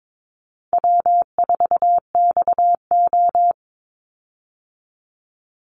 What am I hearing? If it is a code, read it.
Morse code, heard as a keyed tone, about 22 wpm, 690 Hz. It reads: W4XO
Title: Easy Call Signs Repeated 2x - 22wpm